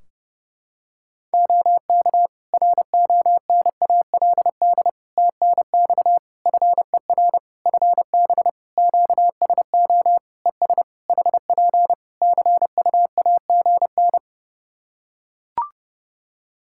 Morse code, heard as a keyed tone, about 30 wpm, 700 Hz. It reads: OK RONALD TNX FER FB QSO ES HP CUAGN
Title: Practice QSO - Exchange 4 - Ending - 30wpm